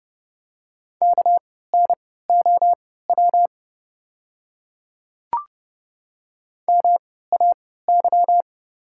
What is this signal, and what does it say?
Morse code, heard as a keyed tone, about 30 wpm, spaced as for 17 wpm, 690 Hz. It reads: KNOW MAY
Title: 3x Character Spacing - Top 200 Words - 30wpm